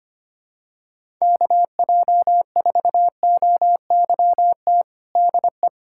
Morse code, heard as a keyed tone, about 25 wpm, 700 Hz. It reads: KJ4OYT DE
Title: USA Callsign DE Callsign - 6 Characters - 25wpm